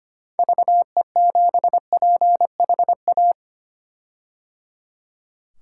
Morse code, heard as a keyed tone, about 25 wpm, 700 Hz. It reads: VE7PHA
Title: K1USN SST QSO Practice - 25wpm